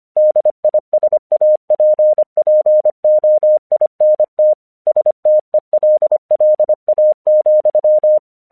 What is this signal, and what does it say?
Morse code, heard as a keyed tone, about 25 wpm, 610 Hz. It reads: DISAPPOINT STELLA,
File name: 1 English 25wpm / AQC19